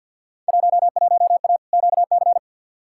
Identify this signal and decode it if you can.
Morse code, heard as a keyed tone, about 52 wpm, 700 Hz. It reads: 11A QC
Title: Field Day Exchanges Encore - 50wpm